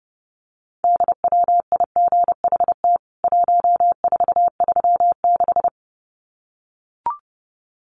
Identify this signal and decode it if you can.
Morse code, heard as a keyed tone, about 30 wpm, 700 Hz. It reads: DWIGHT 1436
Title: CWT Exchange 30wpm